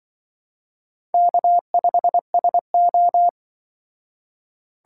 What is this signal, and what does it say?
Morse code, heard as a keyed tone, about 25 wpm, 700 Hz. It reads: K5SO
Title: Speed Racing - USA 1x2 Call Signs 20wpm